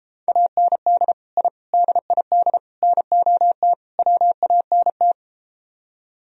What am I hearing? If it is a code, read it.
Morse code, heard as a keyed tone, about 33 wpm, 700 Hz. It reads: AND I DID NOT WANT